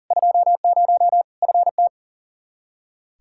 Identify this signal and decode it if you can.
Morse code, heard as a keyed tone, about 40 wpm, 700 Hz. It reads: 20 FT
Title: K1USN SST QSO Practice 40wpm